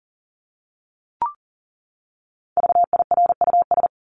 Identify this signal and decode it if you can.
Morse code, heard as a keyed tone, about 40 wpm, 700 Hz. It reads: VIRUS
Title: Top 1251-1500 Words 40wpm